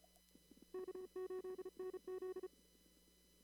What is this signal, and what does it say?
Morse code, heard as a keyed tone, about 34 wpm, 370 Hz. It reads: K8NZ
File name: CW / kt5j wpx cw-op-k5pi